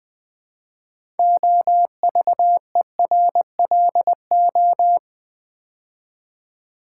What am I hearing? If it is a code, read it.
Morse code, heard as a keyed tone, about 21 wpm, 700 Hz. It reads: OVERLO
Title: Build A Word - 3 Letters At A Time - 8-Letter Words - 20wpm